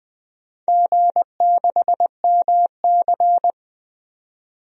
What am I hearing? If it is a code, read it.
Morse code, heard as a keyed tone, about 20 wpm, 700 Hz. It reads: G6MC
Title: DXCC-100 Call Signs - 4 Characters - 20wpm